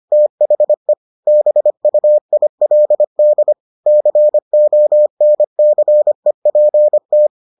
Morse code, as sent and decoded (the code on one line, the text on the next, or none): - .... / -... ..- .. .-.. -.. / -.-. --- -. -.-. .--. -
THE BUILD CONCEPT